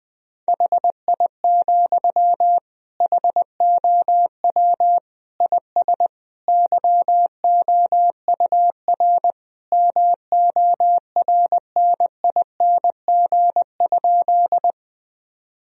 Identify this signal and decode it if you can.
Morse code, heard as a keyed tone, about 19 wpm, 700 Hz. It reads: HI, HOW IS YOUR MORNING?
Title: Sentences from Top 1000 Words - Review Part 3 - 20wpm